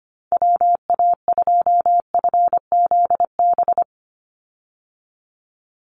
Morse code, heard as a keyed tone, about 25 wpm, 700 Hz. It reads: WA2FZB
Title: Call Signs Repeated 2x - 25wpm